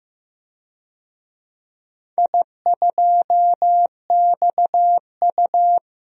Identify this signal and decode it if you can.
Morse code, heard as a keyed tone, about 15 wpm, 700 Hz. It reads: I2XU
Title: DXCC-100 Call Signs - 4 Characters - 15wpm